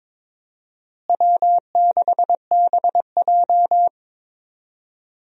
Morse code, heard as a keyed tone, about 21 wpm, 700 Hz. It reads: W6BJ
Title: USA Call Signs - 4 Characters - Repeated 2x - 22wpm